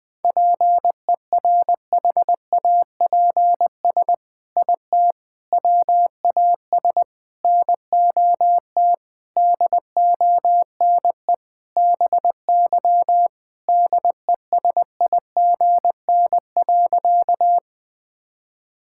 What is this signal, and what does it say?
Morse code, heard as a keyed tone, about 20 wpm, 700 Hz. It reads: PERHAPS IT WAS NOT DONE BY DESIGN.
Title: Sentences from Top 1000 Words - Review Part 5 - 20wpm